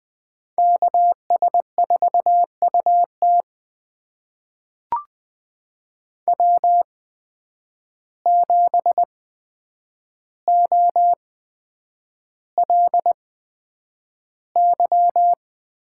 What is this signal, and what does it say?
Morse code, heard as a keyed tone, about 20 wpm, spaced as for 9 wpm, 700 Hz. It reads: KS4UT W7OLY
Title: USA Call Signs - 8x Character Spacing - 20wpm